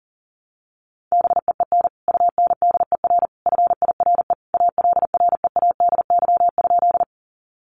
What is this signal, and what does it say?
Morse code, heard as a keyed tone, about 40 wpm, 700 Hz. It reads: BEEN UNDER FIRE ALREADY?
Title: Sentences from Top 700 Words 40wpm